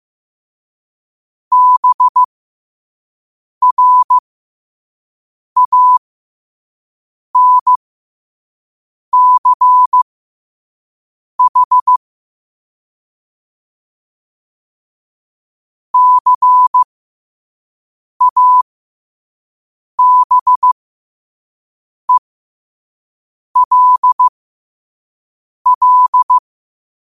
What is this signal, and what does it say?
Morse code, heard as a keyed tone, about 15 wpm, spaced as for 5 wpm, 1000 Hz. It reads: BRANCH CABELL
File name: qotdfiles / 5wpm